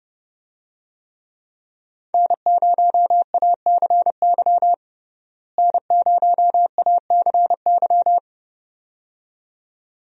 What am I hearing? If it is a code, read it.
Morse code, heard as a keyed tone, about 30 wpm, 700 Hz. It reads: N0ACY N0ACY
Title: USA Call Signs - 5 Characters - Repeated 2x - 30wpm